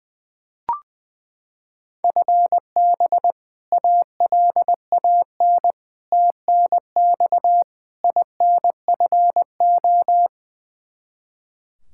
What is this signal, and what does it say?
Morse code, heard as a keyed tone, about 20 wpm, 700 Hz. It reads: FB ALAN TNX INFO
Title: Practice QSO - Exchange 3 - More Info - 20wpm